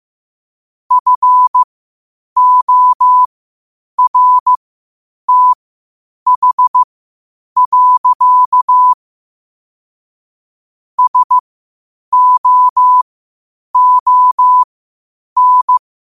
Morse code, heard as a keyed tone, about 15 wpm, spaced as for 8 wpm, 1000 Hz. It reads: FORTH. SOON
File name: 8wpm